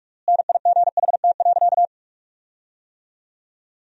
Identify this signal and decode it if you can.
Morse code, heard as a keyed tone, about 46 wpm, 700 Hz. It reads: NIGHT.
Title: 3x Word-Spacing - Sentences from Top 500 Words - 45wpm